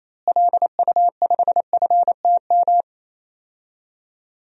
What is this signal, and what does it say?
Morse code, heard as a keyed tone, about 28 wpm, 700 Hz. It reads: LU5FTM
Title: DXCC-100 Call Signs - 6 Characters - Repeated 2x - 28wpm